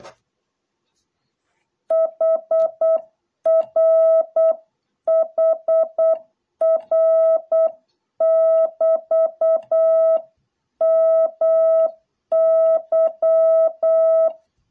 Morse code, heard as a keyed tone, about 8 wpm, 650 Hz. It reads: HR HR<BT>MY